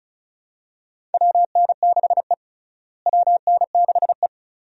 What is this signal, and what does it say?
Morse code, heard as a keyed tone, about 36 wpm, 700 Hz. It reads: WN6E WN6E
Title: USA Call Signs - 4 Characters - Repeated 2x - 35wpm